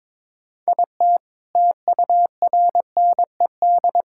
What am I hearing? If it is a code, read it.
Morse code, heard as a keyed tone, about 22 wpm, 700 Hz. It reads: IT TURNED
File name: practice / 22wpm / Sentences with 3-Word Phrases - 22wpm